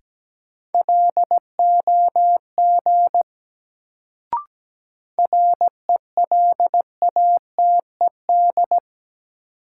Morse code, heard as a keyed tone, about 17 wpm, 700 Hz. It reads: LOG RELATED
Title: Sentences from Top 1400 Words - New Word Drill Alt - 17wpm